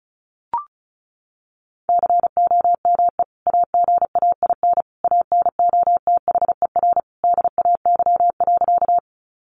Morse code, heard as a keyed tone, about 34 wpm, 700 Hz. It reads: COME AGAIN ANOTHER DAY.
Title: Sentences from Top 600 Words 35wpm